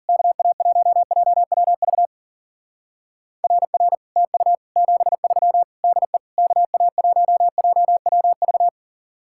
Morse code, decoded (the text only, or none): KA1JWV RR TU 73 DE KA1JWV